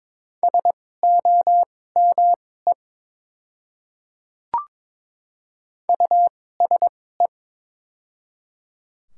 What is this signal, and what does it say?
Morse code, heard as a keyed tone, about 22 wpm, spaced as for 16 wpm, 700 Hz. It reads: SOME USE
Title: 2x Character Spacing - Top 100 Words - 22wpm